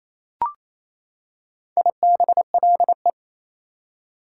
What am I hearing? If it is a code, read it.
Morse code, heard as a keyed tone, about 28 wpm, 700 Hz. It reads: IBLE
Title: Prefixes and Suffixes 28wpm